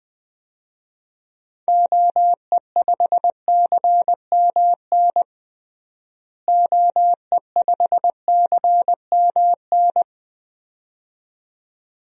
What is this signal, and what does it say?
Morse code, heard as a keyed tone, about 20 wpm, 700 Hz. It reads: OE5CMN OE5CMN
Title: DXCC-100 Call Signs - 6 Characters - Repeated 2x - 20wpm